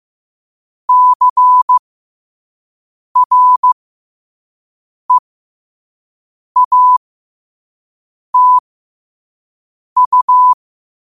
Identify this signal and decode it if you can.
Morse code, heard as a keyed tone, about 15 wpm, spaced as for 5 wpm, 1000 Hz. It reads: CREATU
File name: qotdfiles / 5wpm